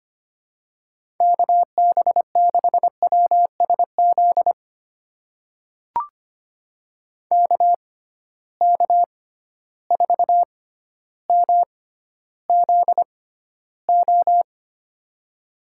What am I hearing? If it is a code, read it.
Morse code, heard as a keyed tone, about 25 wpm, spaced as for 13 wpm, 700 Hz. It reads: KB6WSZ KK4MZO
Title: USA Call Signs - 6 Characters - 6x Character Spacing - 25wpm